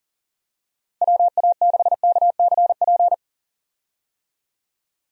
Morse code, heard as a keyed tone, about 40 wpm, 700 Hz. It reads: WA6KCP
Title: USA Call Signs - 6 Characters - 40wpm